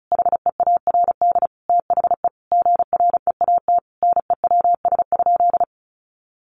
Morse code, heard as a keyed tone, about 34 wpm, 700 Hz. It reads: HEARD THE GREAT NEWS?
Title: Sentences from Top 800 Words 35wpm